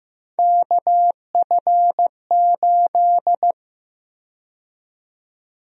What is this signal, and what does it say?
Morse code, heard as a keyed tone, about 15 wpm, 700 Hz. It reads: KF8
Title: USA Call Sign Prefixes - 15wpm